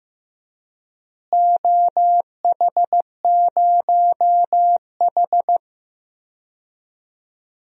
Morse code, heard as a keyed tone, about 15 wpm, 700 Hz. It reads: OH0H
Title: DXCC-100 Call Signs - 4 Characters - Repeated 2x - 15wpm